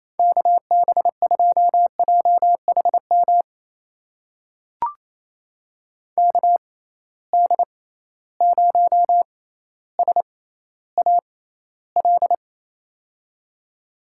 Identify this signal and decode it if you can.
Morse code, heard as a keyed tone, about 28 wpm, spaced as for 14 wpm, 700 Hz. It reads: KB2JHM KD0SAL
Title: USA Call Signs - 6 Characters - 6x Character Spacing - 28wpm